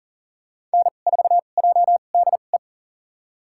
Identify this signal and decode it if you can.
Morse code, heard as a keyed tone, about 40 wpm, spaced as for 29 wpm, 700 Hz. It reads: N4JDE
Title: USA Call Signs - 5 Characters - 2x Character Spacing - 40wpm